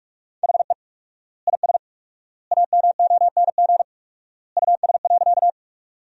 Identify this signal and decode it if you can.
Morse code, heard as a keyed tone, about 41 wpm, 700 Hz. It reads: HE IS AMONG US.
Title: 4x Word-Spacing - Sentences from Top 500 Words - 45wpm